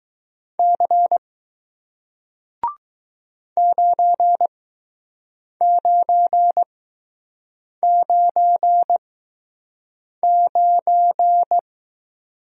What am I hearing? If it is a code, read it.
Morse code, heard as a keyed tone, about 20 wpm, spaced as for 6 wpm, 700 Hz. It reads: C 9999